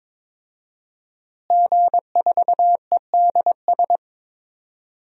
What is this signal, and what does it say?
Morse code, heard as a keyed tone, about 22 wpm, 700 Hz. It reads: G4EDS
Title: DXCC-100 Call Signs - 5 Characters - Repeated 2x - 22wpm